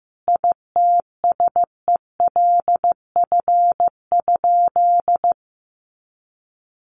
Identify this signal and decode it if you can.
Morse code, heard as a keyed tone, about 15 wpm, 700 Hz. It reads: ITSELF?
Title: Sentences from Top 800 Words 15wpm